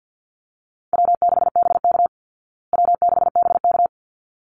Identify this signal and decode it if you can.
Morse code, heard as a keyed tone, about 48 wpm, 700 Hz. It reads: W6BK W6BK
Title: Call Signs Repeated 3x - 50wpm